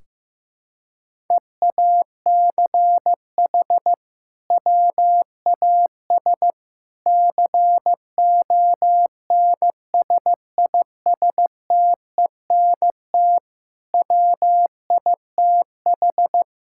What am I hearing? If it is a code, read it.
Morse code, heard as a keyed tone, about 15 wpm, 700 Hz. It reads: EACH WAS CONSISTENT WITH
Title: Sentences from Top 1600 Words 15wpm